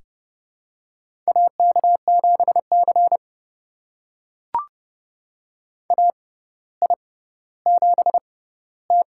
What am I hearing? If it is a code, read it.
Morse code, heard as a keyed tone, about 32 wpm, spaced as for 16 wpm, 700 Hz. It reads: AK7C AI7T
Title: USA Call Signs - 4 Characters - 6x Character Spacing - 30wpm